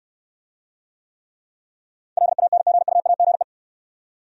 Morse code, heard as a keyed tone, about 63 wpm, 700 Hz. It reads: LANGUAGE